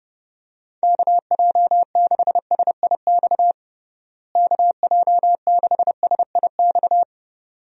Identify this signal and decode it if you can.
Morse code, heard as a keyed tone, about 32 wpm, 700 Hz. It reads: KJ6SIX KJ6SIX
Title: USA Call Signs - 6 Characters - Repeated 2x - 30wpm